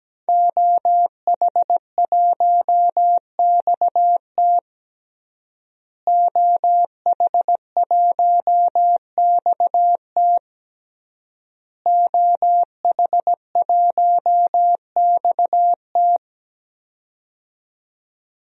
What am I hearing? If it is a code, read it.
Morse code, heard as a keyed tone, about 17 wpm, 700 Hz. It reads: OH1XT OH1XT OH1XT
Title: DXCC-100 Call Signs - 5 Characters - Repeated 3x - 17wpm